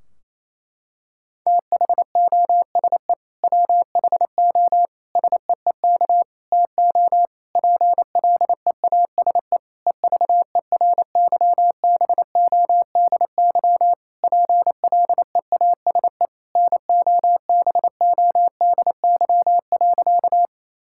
Morse code, heard as a keyed tone, about 28 wpm, 700 Hz. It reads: THOSE WHO SEEK TO PLEASE EVERYBODY PLEASE NOBODY.
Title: Sentences from Top 1600 Words 28wpm